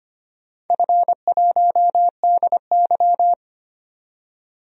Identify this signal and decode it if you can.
Morse code, heard as a keyed tone, about 25 wpm, 700 Hz. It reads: F1DY
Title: DXCC-100 Call Signs - 4 Characters - 25wpm